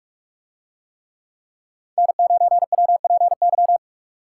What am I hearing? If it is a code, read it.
Morse code, heard as a keyed tone, about 44 wpm, 700 Hz. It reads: N9WPY